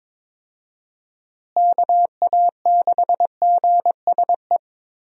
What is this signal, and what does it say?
Morse code, heard as a keyed tone, about 22 wpm, 700 Hz. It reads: KA6GSE
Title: USA Call Signs - 6 Characters - Repeated 3x - 22wpm